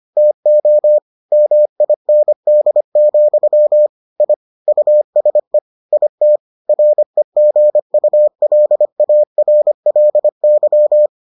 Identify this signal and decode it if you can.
Morse code, heard as a keyed tone, about 25 wpm, 600 Hz. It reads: TO MIND, I USE IT REGULARLY